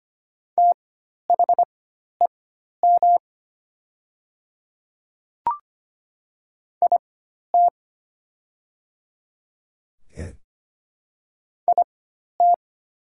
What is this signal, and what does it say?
Morse code, heard as a keyed tone, about 25 wpm, spaced as for 12 wpm, 700 Hz. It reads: THEM IT IT
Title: 4x Character Spacing - Top 100 Words - 25wpm